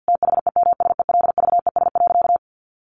Morse code, heard as a keyed tone, about 50 wpm, 700 Hz. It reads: THEMSELVES.
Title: Sentences from Top 800 Words 50wpm